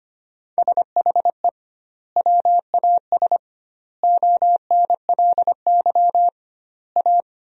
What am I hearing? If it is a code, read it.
Morse code, heard as a keyed tone, about 25 wpm, 700 Hz. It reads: SHE WAS ONLY A
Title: 2x Word-Spacing - Sentences from Top 400 Words - 25wpm